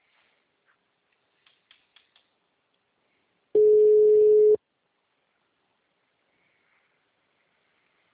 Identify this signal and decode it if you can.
Ringback tone (Europe, 425 Hz).
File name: out- -2001- - - .156